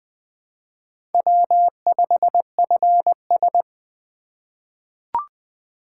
Morse code, heard as a keyed tone, about 21 wpm, 700 Hz. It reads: W5FS
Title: USA Call Signs - 4 Characters - 2x Character Spacing - 20wpm